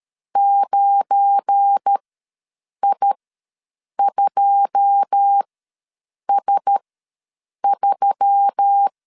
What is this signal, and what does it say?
Morse code, heard as a keyed tone, about 13 wpm, spaced as for 7 wpm, 800 Hz. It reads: 9I2S3
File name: LEZIONI / 9 28 ws 12wpm 3354